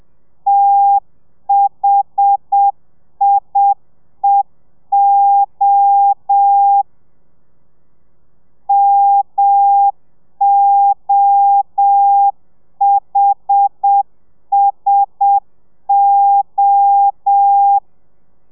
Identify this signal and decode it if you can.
Morse code, heard as a keyed tone, about 7 wpm, 800 Hz. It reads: THIEO MOHSO
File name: telegrafia / i2tfj / Lez1 30